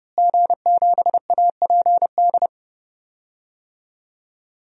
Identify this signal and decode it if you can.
Morse code, heard as a keyed tone, about 30 wpm, 700 Hz. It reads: G7APD
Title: DXCC-100 Call Signs - 5 Characters - Repeated 3x - 30wpm